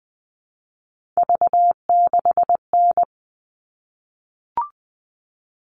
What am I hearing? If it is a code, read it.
Morse code, heard as a keyed tone, about 20 wpm, 700 Hz. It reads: V6N
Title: All Cut Numbers - 3-Digit 20wpm